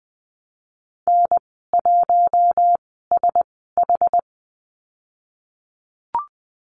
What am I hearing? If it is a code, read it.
Morse code, heard as a keyed tone, about 21 wpm, spaced as for 15 wpm, 700 Hz. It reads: N1SH
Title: 193 Lesson 31g - Letter v - Call Signs 20wpm